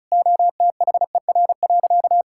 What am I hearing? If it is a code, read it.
Morse code, heard as a keyed tone, about 34 wpm, 700 Hz. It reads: OTHER.
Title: Sentences from Top 1000 Words - Review Part 5 - 35wpm